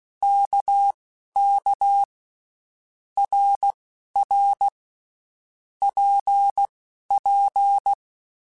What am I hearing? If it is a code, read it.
Morse code, heard as a keyed tone, about 16 wpm, spaced as for 12 wpm, 800 Hz. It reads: KK RR PP
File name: Basic Letter Groups By7Groups / RKPX / RKPX@10